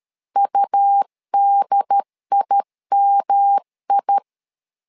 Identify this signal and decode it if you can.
Morse code, heard as a keyed tone, about 13 wpm, 800 Hz. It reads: UDIMI
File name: LEZIONI / 3 MWRO / 3 MWR 12wpm 1711